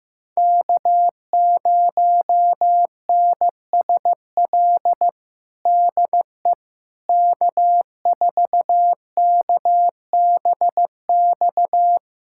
Morse code, heard as a keyed tone, about 15 wpm, 700 Hz. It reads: K0NSL DE K4KBX
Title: USA Callsign DE Callsign - 5 Characters - 15wpm